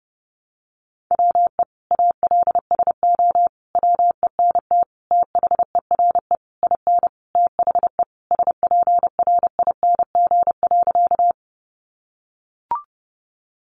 Morse code, heard as a keyed tone, about 28 wpm, 700 Hz. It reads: WE ALSO WENT THERE IN THE SPRING.
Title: Sentences from Top 1200 Words 30wpm